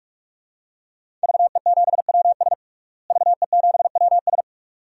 Morse code, heard as a keyed tone, about 46 wpm, 700 Hz. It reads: VE7WS VE7WS
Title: DXCC-100 Call Signs - 5 Characters - Repeated 2x - 45wpm